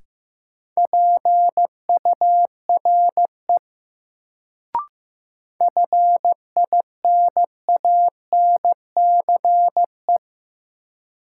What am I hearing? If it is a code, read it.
Morse code, heard as a keyed tone, about 15 wpm, 700 Hz. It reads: PURE FINANCE
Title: Sentences from Top 1500 Words - New Word Drill Alt - 15wpm